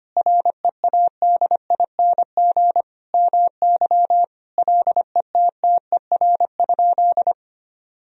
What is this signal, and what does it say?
Morse code, heard as a keyed tone, about 25 wpm, 700 Hz. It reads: READING MY LETTER?
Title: Sentences from Top 1000 Words - Review Part 3 - 25wpm